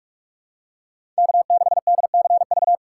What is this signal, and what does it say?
Morse code, heard as a keyed tone, about 44 wpm, 690 Hz. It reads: K6DCV
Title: USA Call Signs - 5 Characters - 4x Character Spacing - 45wpm